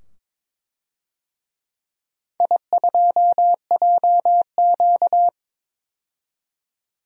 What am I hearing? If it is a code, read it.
Morse code, heard as a keyed tone, about 22 wpm, 700 Hz. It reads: I2JQ
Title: DXCC-100 Call Signs - 4 Characters - 22wpm